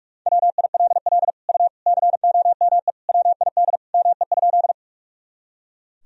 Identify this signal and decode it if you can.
Morse code, heard as a keyed tone, about 45 wpm, 700 Hz. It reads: WILL U COME WID ME?
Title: Sentences from Top 200 Words and Abbreviations 45wpm